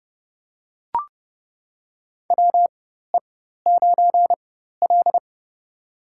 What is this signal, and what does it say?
Morse code, heard as a keyed tone, about 30 wpm, spaced as for 14 wpm, 700 Hz. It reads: WE9L
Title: USA Call Signs - 4 Characters - 4x Character Spacing - 30wpm